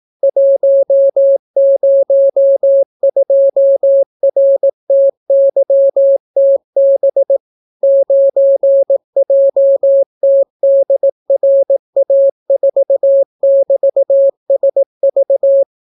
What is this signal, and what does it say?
Morse code, heard as a keyed tone, about 18 wpm, 550 Hz. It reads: 102RTYTB 9JTDRA4<BT>SV